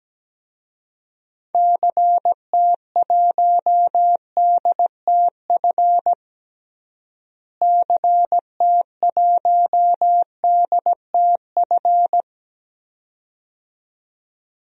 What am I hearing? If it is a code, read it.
Morse code, heard as a keyed tone, about 17 wpm, 700 Hz. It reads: CT1DTF CT1DTF
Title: DXCC-100 Call Signs - 6 Characters - Repeated 2x - 17wpm